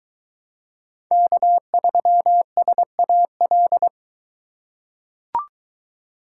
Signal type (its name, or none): Morse code